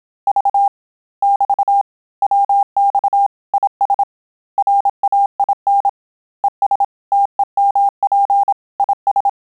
Morse code, heard as a keyed tone, about 27 wpm, 800 Hz. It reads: V <BT> WX IS RAIN ES TEMP IS
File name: Telegraphy-group / QSO CW 25 PPM